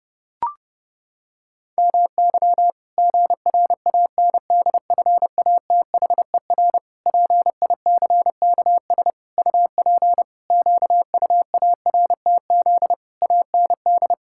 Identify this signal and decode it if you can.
Morse code, heard as a keyed tone, about 30 wpm, 700 Hz. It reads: MY GRANDFATHER PICKS UP QUARTZ AND
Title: Pangram Challenge v2 - 30wpm